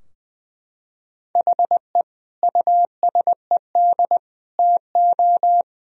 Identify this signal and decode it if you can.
Morse code, heard as a keyed tone, about 21 wpm, 700 Hz. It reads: HE USED TO
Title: Sentences from Top 1000 Words - Review Part 4 - 20wpm